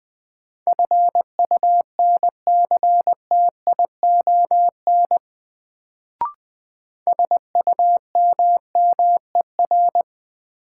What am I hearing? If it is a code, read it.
Morse code, heard as a keyed tone, about 20 wpm, 700 Hz. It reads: FUNCTION SUMMER
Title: Sentences from Top 1100 Words - New Word Drill Alt - 20wpm